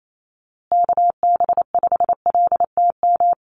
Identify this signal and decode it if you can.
Morse code, heard as a keyed tone, about 28 wpm, 700 Hz. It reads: KB5LTM
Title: USA Call Signs Repeated 2x 28wpm